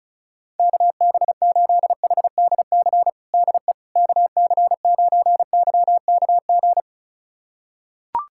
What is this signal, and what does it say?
Morse code, heard as a keyed tone, about 34 wpm, 700 Hz. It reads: KB8HDC DE KC9YKG
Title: USA Callsign DE Callsign - 6 Characters - 35wpm